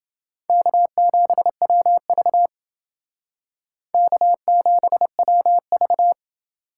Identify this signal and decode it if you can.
Morse code, heard as a keyed tone, about 28 wpm, 700 Hz. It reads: K7WV K7WV